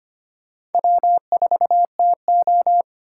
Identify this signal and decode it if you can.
Morse code, heard as a keyed tone, about 33 wpm, 700 Hz. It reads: W4TO